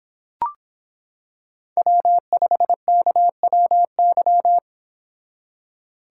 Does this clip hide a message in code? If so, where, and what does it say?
Yes: Morse code from 1.8 to 4.6 s, heard as a keyed tone, about 26 wpm, 700 Hz. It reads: W5KWY